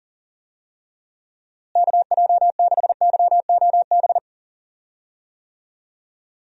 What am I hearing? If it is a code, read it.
Morse code, heard as a keyed tone, about 38 wpm, 700 Hz. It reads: KJ6YOB
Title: USA Call Signs - 6 Characters - Repeated 2x - 40wpm